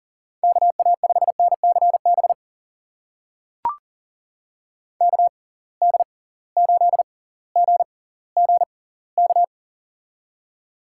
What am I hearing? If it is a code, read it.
Morse code, heard as a keyed tone, about 39 wpm, spaced as for 20 wpm, 700 Hz. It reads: KA5NCB KD8GGK
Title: USA Call Signs - 6 Characters - 6x Character Spacing - 40wpm